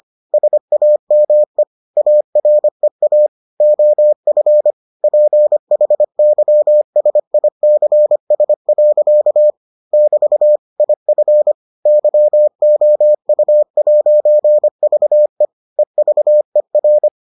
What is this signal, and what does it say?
Morse code, heard as a keyed tone, about 25 wpm, 600 Hz. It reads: SAME AREA OF PHYSICS. <BT> IF YOU'VE EVER